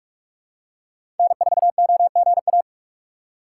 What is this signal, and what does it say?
Morse code, heard as a keyed tone, about 43 wpm, 700 Hz. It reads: N4OGA
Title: USA Call Signs - 5 Characters - 3x Character Spacing - 45wpm